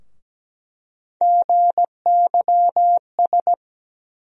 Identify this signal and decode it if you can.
Morse code, heard as a keyed tone, about 17 wpm, 700 Hz. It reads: GYS
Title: USA Call Sign Postfixes - 3 Letters 17wpm